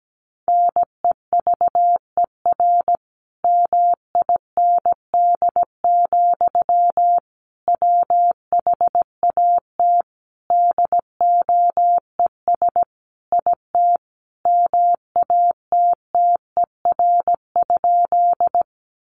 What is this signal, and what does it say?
Morse code, heard as a keyed tone, about 17 wpm, 700 Hz. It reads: NEVER MIND, WHAT DOES IT MATTER?
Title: Sentences from Top 700 Words 17wpm